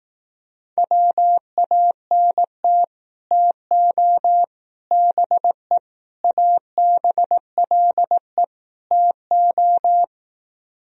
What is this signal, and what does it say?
Morse code, heard as a keyed tone, about 18 wpm, 700 Hz. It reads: WANT TO BE ABLE TO
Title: Speed-Racing - Sets of 5 Words 15wpm